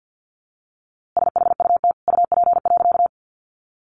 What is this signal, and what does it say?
Morse code, heard as a keyed tone, about 50 wpm, 700 Hz. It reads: SHUT UP.
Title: A Sound Of Thunder 50wpm